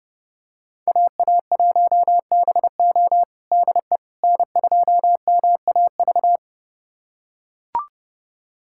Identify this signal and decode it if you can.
Morse code, heard as a keyed tone, about 32 wpm, 700 Hz. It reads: AA1BO DE N2MAV
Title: USA Callsign DE Callsign - 5 Characters - 30wpm